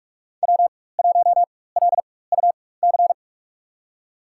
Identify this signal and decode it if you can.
Morse code, heard as a keyed tone, about 42 wpm, spaced as for 20 wpm, 700 Hz. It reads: W1LUC
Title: USA Call Signs - 4x Character Spacing - 45wpm